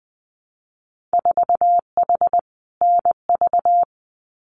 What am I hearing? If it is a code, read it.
Morse code, heard as a keyed tone, about 20 wpm, 700 Hz. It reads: 4H NV
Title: Winter Field Day Exchanges Encore 20wpm